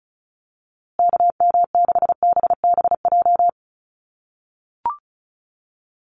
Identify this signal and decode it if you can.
Morse code, heard as a keyed tone, about 34 wpm, 700 Hz. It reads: KM6BBJ
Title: USA Call Signs Repeated 3x 35wpm